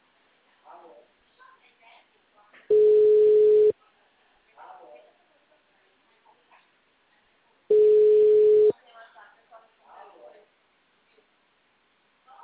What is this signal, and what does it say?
Ringback tone (Europe, 425 Hz), with speech.
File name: out- -2006- - - .6874